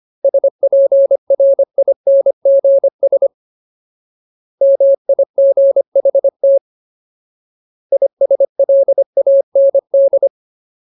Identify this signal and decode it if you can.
Morse code, heard as a keyed tone, about 25 wpm, 550 Hz. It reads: SPRINGS MIGHT ISLAND
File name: Ultimate Word List / 25WPM Ultimate